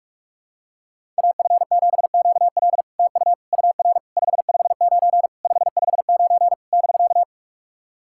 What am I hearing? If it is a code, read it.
Morse code, heard as a keyed tone, about 44 wpm, 660 Hz. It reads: AF7QL TU UR 559 559 <BK>
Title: SOTA QSO Practice 45wpm